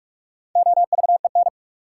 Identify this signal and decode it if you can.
Morse code, heard as a keyed tone, about 46 wpm, 700 Hz. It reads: OVEN
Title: Sentences from Top 1000 Words - Review Part 4 - New Word Drill - 45wpm